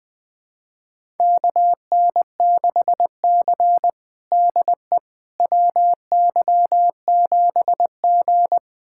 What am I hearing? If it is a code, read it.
Morse code, heard as a keyed tone, about 20 wpm, 700 Hz. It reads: KN6C DE WY7G
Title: USA Callsign DE Callsign - 4 Characters - 20wpm